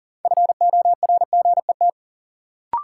Morse code, heard as a keyed tone, about 40 wpm, 700 Hz. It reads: FORGET
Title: Sentences from Top 1200 Words - New Word Drill - 40wpm